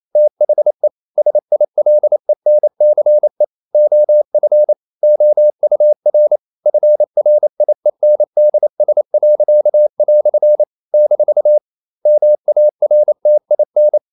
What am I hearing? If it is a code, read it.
Morse code, heard as a keyed tone, about 28 wpm, 600 Hz. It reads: THE SILENCE OF OUR FRIENDS." - MARTIN
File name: CW / Practice / EnglishQuotations0128